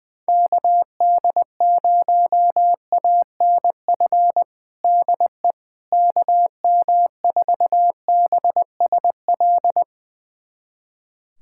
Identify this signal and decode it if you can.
Morse code, heard as a keyed tone, about 20 wpm, 700 Hz. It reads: KD0ANF DE KM4BSL
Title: USA Callsign DE Callsign - 6 Characters - 20wpm